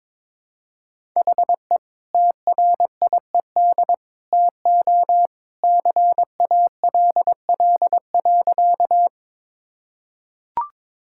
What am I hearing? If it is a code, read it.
Morse code, heard as a keyed tone, about 21 wpm, 700 Hz. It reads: HE TRIED TO CALL.
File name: practice / 22wpm / Sentences with 2-Word Phrases - 22wpm